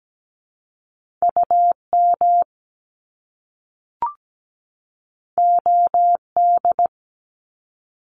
Morse code, heard as a keyed tone, about 17 wpm, 700 Hz. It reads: UM OD
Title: Top Two Letter Combinations in Words 17wpm